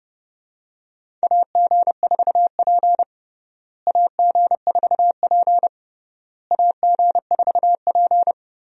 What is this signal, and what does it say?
Morse code, heard as a keyed tone, about 30 wpm, 700 Hz. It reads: AG4P AG4P AG4P
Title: USA Call Signs - 4 Characters - Repeated 3x - 30wpm